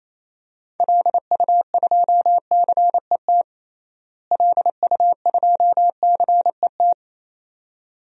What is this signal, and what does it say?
Morse code, heard as a keyed tone, about 28 wpm, 700 Hz. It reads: LU2CET LU2CET
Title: DXCC-100 Call Signs - 6 Characters - Repeated 2x - 28wpm